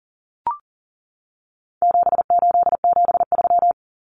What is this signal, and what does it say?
Morse code, heard as a keyed tone, about 40 wpm, 700 Hz. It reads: 7873
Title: Common Cut Numbers - 4-Digit 40wpm